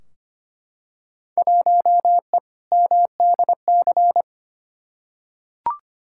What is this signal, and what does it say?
Morse code, heard as a keyed tone, about 25 wpm, 700 Hz. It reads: 1E MDC
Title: Field Day Exchanges - 25wpm